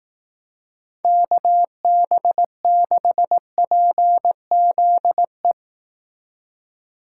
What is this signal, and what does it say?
Morse code, heard as a keyed tone, about 19 wpm, 700 Hz. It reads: KB6PZE